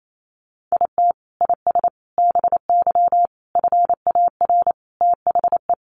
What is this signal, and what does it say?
Morse code, heard as a keyed tone, about 28 wpm, 700 Hz. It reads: IT IS BY FAR THE
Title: Sentences from Top 1200 Words 28wpm